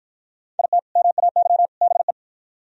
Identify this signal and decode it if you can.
Morse code, heard as a keyed tone, about 52 wpm, 700 Hz. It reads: IT MAY BE